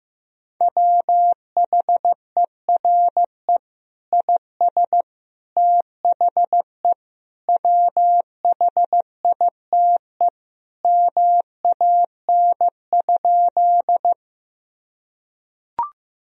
Morse code, heard as a keyed tone, about 16 wpm, 700 Hz. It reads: WHERE IS THE WHITE MAN?
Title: Sentences from Top 1000 Words - Review Part 1 - 15wpm